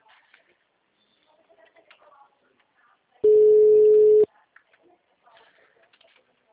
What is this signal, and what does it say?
Ringback tone (Europe, 425 Hz), with speech.
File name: out- -2002- - - .173